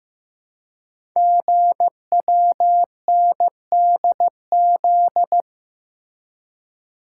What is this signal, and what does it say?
Morse code, heard as a keyed tone, about 15 wpm, 690 Hz. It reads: GWNDZ
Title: Code Groups - 5 Letters - 15wpm